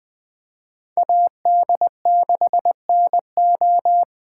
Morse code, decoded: AD6NO

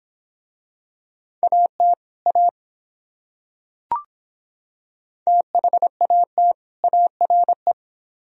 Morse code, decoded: AT A THAT ARE

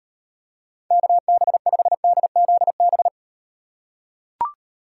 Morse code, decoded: KB5DZB

